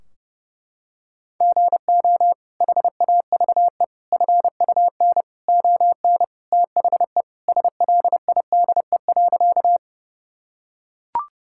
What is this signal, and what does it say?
Morse code, heard as a keyed tone, about 30 wpm, 700 Hz. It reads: GO HAVE FUN ON THE SLIDE.
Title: Sentences from Top 1600 Words 30wpm